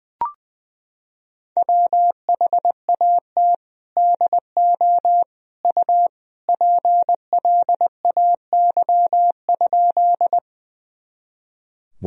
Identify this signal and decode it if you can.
Morse code, heard as a keyed tone, about 20 wpm, 700 Hz. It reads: WHAT DO U PLAY?
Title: Sentences from Top 200 Words and Abbreviations 20wpm